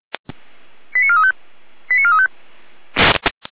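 Five-tone selective call (ZVEI), heard as quick runs of five tones, tones of about 70 ms. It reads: 89435 89435
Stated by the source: ZVEI five-tone call 88019 received on SDR, Feuerwehr Orlamünde.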